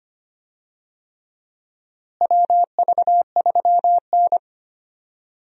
Morse code, heard as a keyed tone, about 25 wpm, 700 Hz. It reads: WV3N
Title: USA Call Signs - 4 Characters - Repeated 3x - 25wpm